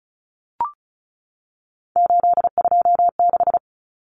Morse code, heard as a keyed tone, about 34 wpm, 700 Hz. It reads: 826
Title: Common Cut Numbers - 3-Digit 35wpm